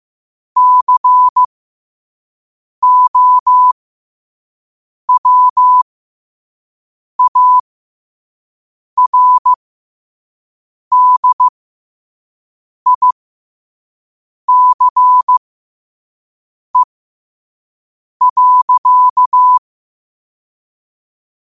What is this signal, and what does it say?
Morse code, heard as a keyed tone, about 15 wpm, spaced as for 5 wpm, 1000 Hz. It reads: COWARDICE.
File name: qotdfiles / 5wpm